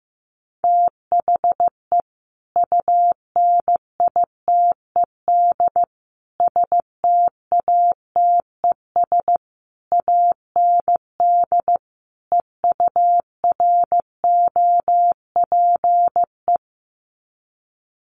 Morse code, heard as a keyed tone, about 14 wpm, 700 Hz. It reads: THE UNITED STATES AND EUROPE
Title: Sets of 5 Words Encore 15wpm